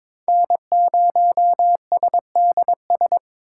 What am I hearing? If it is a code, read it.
Morse code, heard as a keyed tone, about 22 wpm, 700 Hz. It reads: N0SDS
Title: USA Call Signs - 5 Characters - 3x Character Spacing - 22wpm